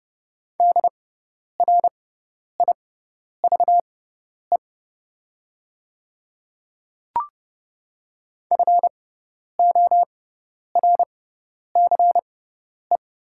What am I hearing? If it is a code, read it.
Morse code, heard as a keyed tone, about 30 wpm, spaced as for 10 wpm, 700 Hz. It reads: DRIVE FORCE
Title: 6x Character Spacing - Top 500 Words - 30wpm